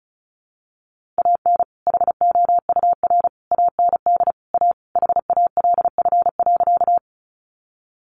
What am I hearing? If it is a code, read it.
Morse code, heard as a keyed tone, about 34 wpm, 700 Hz. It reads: AN HOUR AND A HALF.
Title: Sentences from Top 400 Words 35wpm